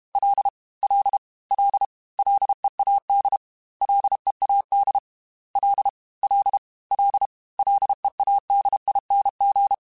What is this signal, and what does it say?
Morse code, heard as a keyed tone, about 32 wpm, 800 Hz. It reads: L L L LEAD LEAD L L L LEADING
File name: Ls / Single L m3s / Single L Prefixe 3x2 Twice @30